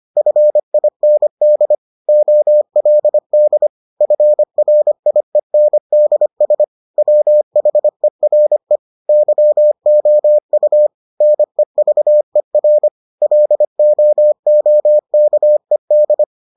Morse code, heard as a keyed tone, about 25 wpm, 600 Hz. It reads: FIND OLD FRIENDS WHERE YOU NEVER LOOKED